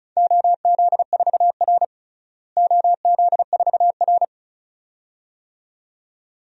Morse code, heard as a keyed tone, about 36 wpm, 700 Hz. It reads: OZ4R OZ4R
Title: DXCC-100 Call Signs - 4 Characters - Repeated 2x - 35wpm